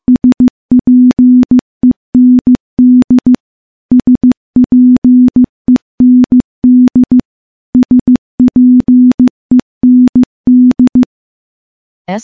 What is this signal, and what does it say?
Morse code, heard as a keyed tone, about 15 wpm, 250 Hz. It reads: SPEND SPEND SPEND
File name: morse / morse-training-15WPM